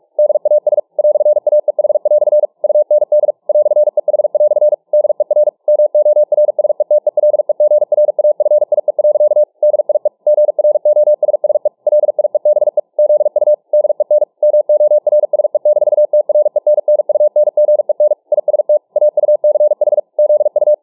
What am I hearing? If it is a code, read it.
Morse code, heard as a keyed tone, about 44 wpm, 600 Hz. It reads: DAS "AEH" UND "EH" DER MORSETELEGRAFIE. DIE GROSSE LIEBE ZU DEN MORSE-TRENNUNGEN IST AUCH ZU